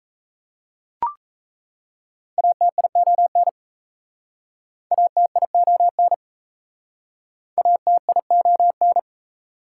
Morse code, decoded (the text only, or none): ATION ATION ATION